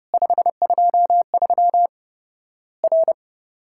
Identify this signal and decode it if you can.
Morse code, heard as a keyed tone, about 30 wpm, 700 Hz. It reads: 523 R
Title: Worked All Europe DX Contest - QTC Exchange - 30wpm